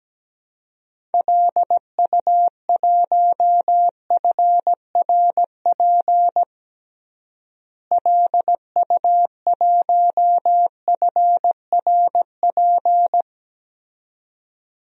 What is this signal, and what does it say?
Morse code, heard as a keyed tone, about 17 wpm, 700 Hz. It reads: LU1FRP LU1FRP
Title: DXCC-100 Call Signs - 6 Characters - Repeated 2x - 17wpm